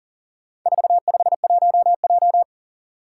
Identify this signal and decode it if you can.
Morse code, heard as a keyed tone, about 40 wpm, 700 Hz. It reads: 451J
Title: Four Challenging Letter-Number Combinations 40wpm